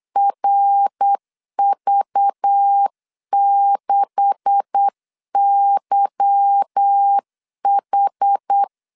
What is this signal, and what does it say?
Morse code, heard as a keyed tone, about 8 wpm, 800 Hz. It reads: RV6YH
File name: LEZIONI / 9 286 8wpm 3425